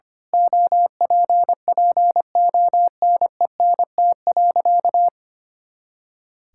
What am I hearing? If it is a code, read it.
Morse code, heard as a keyed tone, about 25 wpm, 700 Hz. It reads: OPPONENT.
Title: Pangram Challenge v2 - 25wpm